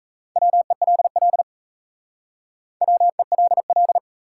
Morse code, heard as a keyed tone, about 37 wpm, 700 Hz. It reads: WELL WELL